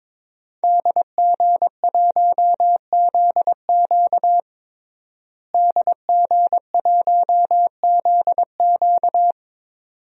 Morse code, heard as a keyed tone, about 22 wpm, 700 Hz. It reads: DG1ZQ DG1ZQ
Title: DXCC-100 Call Signs - 5 Characters - Repeated 2x - 22wpm